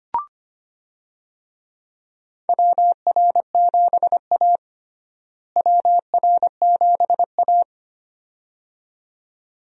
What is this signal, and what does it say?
Morse code, heard as a keyed tone, about 25 wpm, 700 Hz. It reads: WR7A WR7A
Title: USA Call Signs - 4 Characters - Repeated 2x - 25wpm